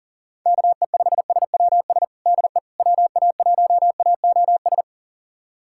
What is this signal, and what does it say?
Morse code, heard as a keyed tone, about 37 wpm, 700 Hz. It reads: KE5SWS DE WA1AOS
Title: USA Callsign DE Callsign - 6 Characters - 40wpm